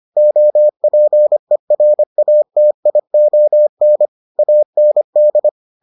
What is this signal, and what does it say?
Morse code, heard as a keyed tone, about 25 wpm, 600 Hz. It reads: OPERATION AND